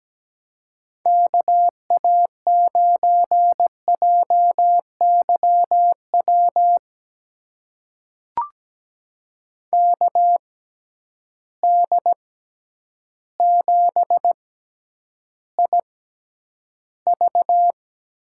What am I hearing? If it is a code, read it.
Morse code, heard as a keyed tone, about 17 wpm, spaced as for 9 wpm, 700 Hz. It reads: KA9JYW KD7IV
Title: USA Call Signs - 6x Character Spacing - 17wpm